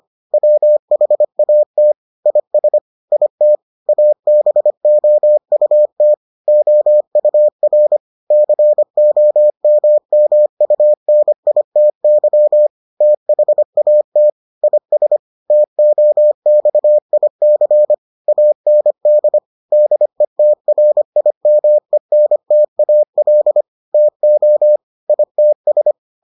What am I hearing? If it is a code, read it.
Morse code, heard as a keyed tone, about 25 wpm, 600 Hz. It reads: WHAT IS IT ABOUT OUR COMMUNITY THAT IS TOXIC AND DETRIMENTAL TO ITS